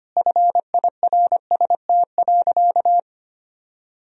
Morse code, heard as a keyed tone, about 25 wpm, 700 Hz. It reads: FIRST.
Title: Sentences from Top 1000 Words - Review Part 1 - 25wpm